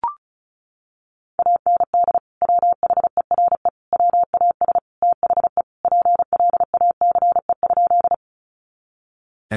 Morse code, heard as a keyed tone, about 34 wpm, 700 Hz. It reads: AND WHERE WAS THE PLACE?
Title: Sentences from Top 200 Words 35wpm